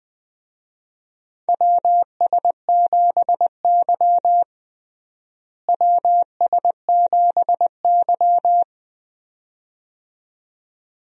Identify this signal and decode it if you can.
Morse code, heard as a keyed tone, about 20 wpm, 700 Hz. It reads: WS7Y WS7Y
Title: USA Call Signs - 4 Characters - Repeated 2x - 20wpm